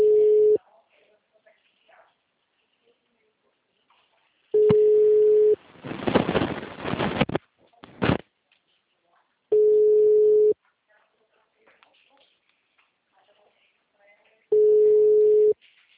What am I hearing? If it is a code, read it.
Ringback tone (Europe, 425 Hz).